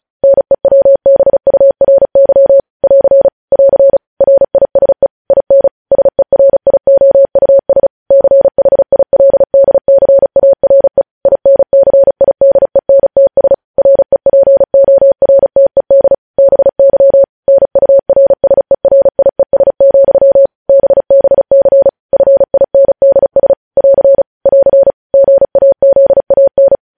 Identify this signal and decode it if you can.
Morse code, heard as a keyed tone, about 36 wpm, 550 Hz. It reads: NEWBURY <AR> <AR> RISE IN SERIOUS CHILDCARE INCIDENTS REPORTED BY NURSERIES, BBC FINDS <AR> <AR> GAZAN
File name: podcast / News headlines-1x-35x35